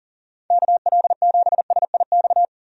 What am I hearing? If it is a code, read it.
Morse code, heard as a keyed tone, about 40 wpm, 700 Hz. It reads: KL7SIX
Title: DXCC-100 Call Signs - 6 Characters - 40wpm